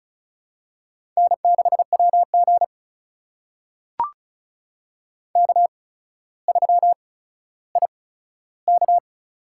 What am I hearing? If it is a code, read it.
Morse code, heard as a keyed tone, about 36 wpm, spaced as for 15 wpm, 700 Hz. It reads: N6WG K3IK
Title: USA Call Signs - 4 Characters - 8x Character Spacing - 35wpm